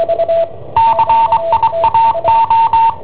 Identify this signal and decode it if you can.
Morse code, heard as a keyed tone, about 22 wpm, 910 Hz. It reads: CIAO